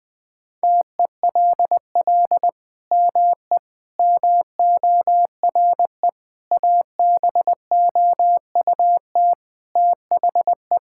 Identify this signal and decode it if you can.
Morse code, heard as a keyed tone, about 20 wpm, 700 Hz. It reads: TELL ME MORE ABOUT THE
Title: Sentences from Top 1000 Words - Review Part 5 - 20wpm